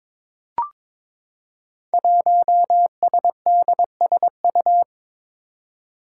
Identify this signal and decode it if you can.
Morse code, heard as a keyed tone, about 22 wpm, 700 Hz. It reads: 1SDSU
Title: Five Challenging Letter-Number Combinations 22wpm